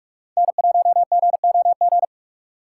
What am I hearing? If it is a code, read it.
Morse code, heard as a keyed tone, about 41 wpm, 700 Hz. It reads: N1GOG
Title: USA Call Signs - 5 Characters - 4x Character Spacing - 45wpm